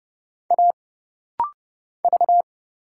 Morse code, heard as a keyed tone, about 29 wpm, 700 Hz. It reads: A V
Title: Single Letter-Number - Rapid-Fire 30wpm